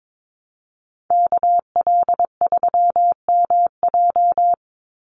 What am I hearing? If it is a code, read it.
Morse code, heard as a keyed tone, about 22 wpm, 700 Hz. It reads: KL3MJ
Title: USA Call Signs 22wpm